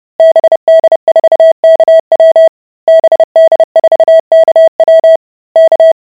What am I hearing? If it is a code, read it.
Morse code, heard as a keyed tone, about 30 wpm, 650 Hz. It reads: BD4KW BD4KW K